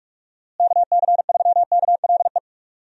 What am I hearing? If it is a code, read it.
Morse code, heard as a keyed tone, about 41 wpm, 700 Hz. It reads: KC3KLE
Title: USA Call Signs - 6 Characters - Repeated 3x - 45wpm